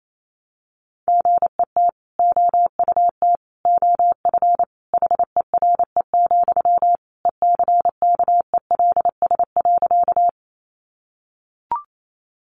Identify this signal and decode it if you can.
Morse code, heard as a keyed tone, about 28 wpm, 700 Hz. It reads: GET OUT OF HERE, ECKELS.
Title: A Sound Of Thunder 28wpm